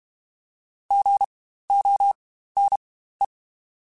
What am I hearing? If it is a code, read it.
Morse code, heard as a keyed tone, about 32 wpm, spaced as for 15 wpm, 800 Hz. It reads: GONE
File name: Spaced Alphas / Spaced Alphas mp3s / Gs sp @30